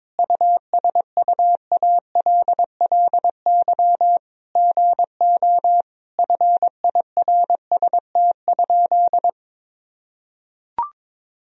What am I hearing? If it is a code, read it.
Morse code, heard as a keyed tone, about 21 wpm, 700 Hz. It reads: USUALLY GO FIRST?
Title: Sentences from Top 1000 Words - Review Part 3 - 22wpm